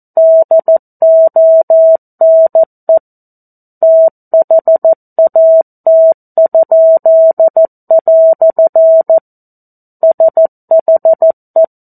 Morse code, heard as a keyed tone, about 14 wpm, 650 Hz. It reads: DONE THAT?" SHE
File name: 02-Chapter-02 14 WPM